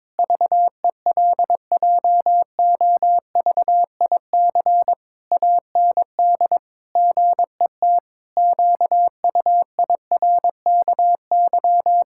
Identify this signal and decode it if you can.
Morse code, heard as a keyed tone, about 22 wpm, 700 Hz. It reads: VELJOVIC AND GET QUIRKY
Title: Pangram Challenge v2 - 22wpm